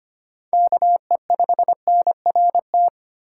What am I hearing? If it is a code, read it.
Morse code, heard as a keyed tone, about 25 wpm, 700 Hz. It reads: KE5NRT
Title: USA Call Signs - 6 Characters - 8x Character Spacing - 25wpm